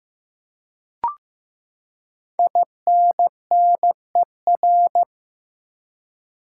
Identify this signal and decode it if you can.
Morse code, heard as a keyed tone, about 15 wpm, 700 Hz. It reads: INNER
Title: Five Letter Words - 15wpm